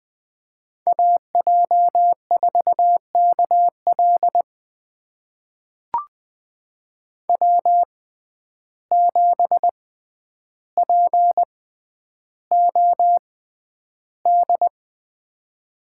Morse code, heard as a keyed tone, about 20 wpm, spaced as for 10 wpm, 700 Hz. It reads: AJ4KL W7POD
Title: USA Call Signs - 5 Characters - 6x Character Spacing - 20wpm